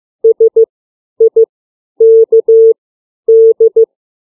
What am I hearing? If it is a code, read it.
Morse code, heard as a keyed tone, about 15 wpm, 450 Hz. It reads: S I K D
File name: CW / alphabet 5